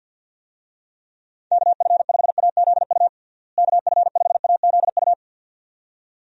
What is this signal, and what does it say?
Morse code, heard as a keyed tone, about 50 wpm, 700 Hz. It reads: KF5AZU KF5AZU
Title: USA Call Signs - 6 Characters - Repeated 2x - 50wpm